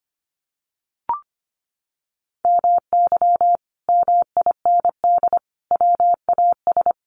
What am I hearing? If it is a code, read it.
Morse code, heard as a keyed tone, about 25 wpm, 700 Hz. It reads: MY MIND WAS
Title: Sentences from Top 700 Words 25wpm